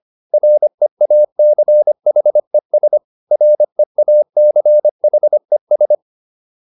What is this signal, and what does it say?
Morse code, heard as a keyed tone, about 25 wpm, 600 Hz. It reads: REACHES REACHES